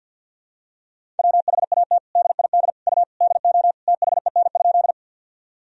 Morse code, heard as a keyed tone, about 51 wpm, 700 Hz. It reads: WHAT DID U DO THEN?
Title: Sentences from Top 100 Words and Abbreviations 50wpm